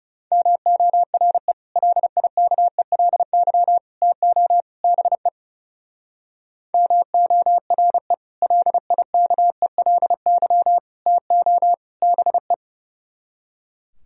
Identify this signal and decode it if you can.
Morse code, heard as a keyed tone, about 36 wpm, 690 Hz. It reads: MORE LIKELY TO BE MORE LIKELY TO BE
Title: Speed-Racing - Sets of 4 Words 30wpm